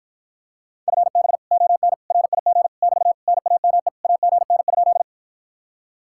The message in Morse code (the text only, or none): FB ON RIG <BT> NAME AGN?